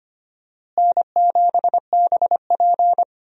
Morse code, heard as a keyed tone, about 25 wpm, 700 Hz. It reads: N7BP
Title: USA Call Signs - 4 Characters - Repeated 2x - 25wpm